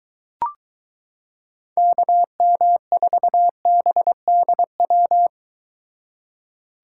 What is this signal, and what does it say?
Morse code, heard as a keyed tone, about 22 wpm, 700 Hz. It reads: KM4BDW